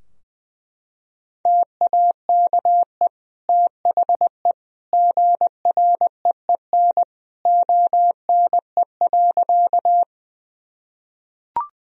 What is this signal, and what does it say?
Morse code, heard as a keyed tone, about 20 wpm, 700 Hz. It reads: TAKE THE GREEN ONE.
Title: Sentences from Top 1000 Words - Review Part 2 - 20wpm